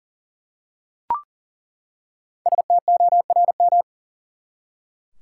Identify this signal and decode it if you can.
Morse code, heard as a keyed tone, about 39 wpm, 700 Hz. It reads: STORM
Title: Five Letter Words - 40wpm